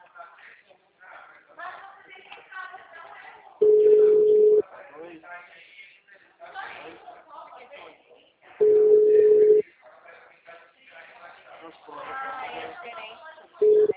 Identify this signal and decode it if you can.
Ringback tone (Europe, 425 Hz), with speech.